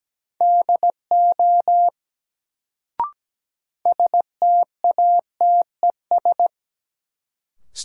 Morse code, heard as a keyed tone, about 17 wpm, 700 Hz. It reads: DO STATES
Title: Sets of 2 Words - New Word Drill - 17wpm